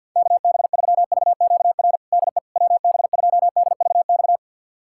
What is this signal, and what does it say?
Morse code, heard as a keyed tone, about 52 wpm, 700 Hz. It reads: KB3VQR DE WB2DVX
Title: USA Callsign DE Callsign - 6 Characters - 50wpm